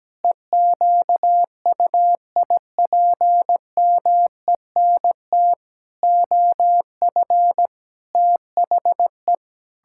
Morse code, heard as a keyed tone, about 17 wpm, 700 Hz. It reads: EQUIPMENT OF THE
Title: Pangram Challenge v2 - 17wpm